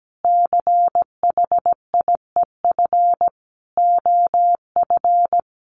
Morse code, heard as a keyed tone, about 17 wpm, 700 Hz. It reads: CHIEF OF
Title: Sets of 5 Words Encore 17wpm